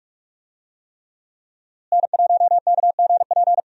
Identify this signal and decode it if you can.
Morse code, heard as a keyed tone, about 46 wpm, 700 Hz. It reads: N1KGP
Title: USA Call Signs - 5 Characters - Repeated 3x - 45wpm